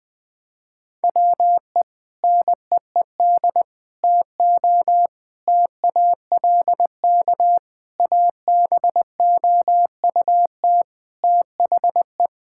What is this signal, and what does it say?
Morse code, heard as a keyed tone, about 21 wpm, 700 Hz. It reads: WE NEED TO TALK ABOUT THE
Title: Sentences with 4-Word Phrases - Spoken Beginning - 20wpm